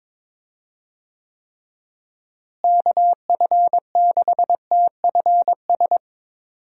Morse code, heard as a keyed tone, about 22 wpm, 700 Hz. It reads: KF6TFS
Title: USA Call Signs - 6 Characters - Repeated 2x - 22wpm